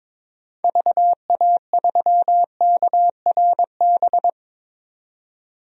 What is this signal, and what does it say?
Morse code, heard as a keyed tone, about 22 wpm, 700 Hz. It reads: VA3KRB
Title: DXCC-100 Call Signs - 6 Characters - 22wpm